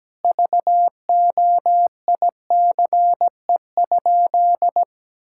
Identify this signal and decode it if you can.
Morse code, heard as a keyed tone, about 17 wpm, 700 Hz. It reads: VOICE?
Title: Sentences from Top 1600 Words 17wpm